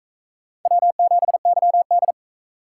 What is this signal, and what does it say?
Morse code, heard as a keyed tone, about 43 wpm, 700 Hz. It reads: W7YD